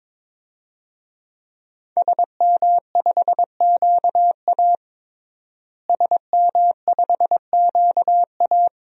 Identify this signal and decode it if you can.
Morse code, heard as a keyed tone, about 22 wpm, 700 Hz. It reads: SM5QA SM5QA
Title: DXCC-100 Call Signs - 5 Characters - Repeated 2x - 22wpm